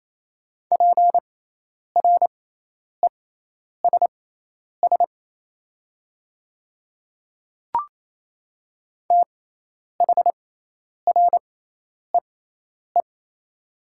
Morse code, heard as a keyed tone, about 27 wpm, spaced as for 10 wpm, 700 Hz. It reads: PRESS THREE
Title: 6x Character Spacing - Top 300 Words - 28wpm